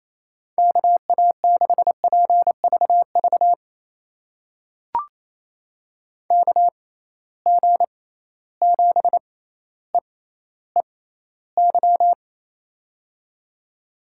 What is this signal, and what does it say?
Morse code, heard as a keyed tone, about 28 wpm, spaced as for 14 wpm, 700 Hz. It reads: KA6PVV KG7EEY
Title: USA Call Signs - 6 Characters - 6x Character Spacing - 28wpm